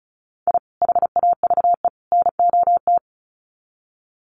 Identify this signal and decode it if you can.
Morse code, heard as a keyed tone, about 34 wpm, 700 Hz. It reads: I HAVE NOT
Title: Sets of 3 Words 35wpm